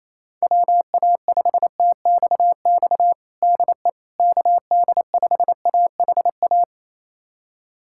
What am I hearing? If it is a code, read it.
Morse code, heard as a keyed tone, about 28 wpm, 700 Hz. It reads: WA5TXX DE KD5AHA
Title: USA Callsign DE Callsign - 6 Characters - 28wpm